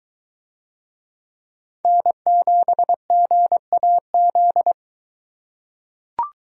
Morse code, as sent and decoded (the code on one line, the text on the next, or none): -. --... --. .- --..
N7GAZ